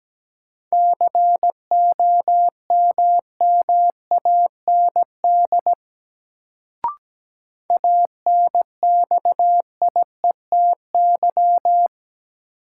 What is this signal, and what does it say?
Morse code, heard as a keyed tone, about 17 wpm, 700 Hz. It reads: COMMAND ANXIETY
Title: Sentences from Top 1600 Words - New Word Drill - 17wpm